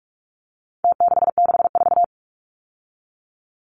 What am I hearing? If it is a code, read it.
Morse code, heard as a keyed tone, about 44 wpm, 700 Hz. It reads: T664
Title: Numbers - Contest Style - 4-Digit 45wpm